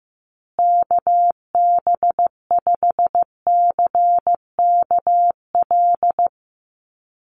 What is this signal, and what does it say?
Morse code, heard as a keyed tone, about 15 wpm, 700 Hz. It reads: KB5CKL
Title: USA Call Signs Repeated 3x 15wpm